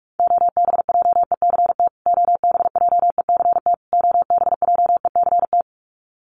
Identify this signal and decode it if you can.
Morse code, heard as a keyed tone, about 44 wpm, 700 Hz. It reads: OBJECT OBJECT OBJECT
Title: Top 500 Words Repeated 45wpm